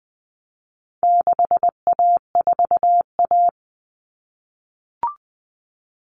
Morse code, heard as a keyed tone, about 20 wpm, 700 Hz. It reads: 6A4A
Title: Common Cut Numbers - 4-Digit 20wpm